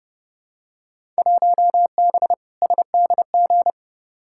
Morse code, heard as a keyed tone, about 32 wpm, 700 Hz. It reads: 1B SDG
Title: Field Day Exchanges Encore - 30wpm